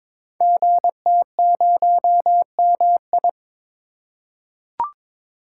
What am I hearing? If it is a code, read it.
Morse code, heard as a keyed tone, about 22 wpm, 700 Hz. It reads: GT0MI
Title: Code Groups - 5 Letter:Numbers - 22wpm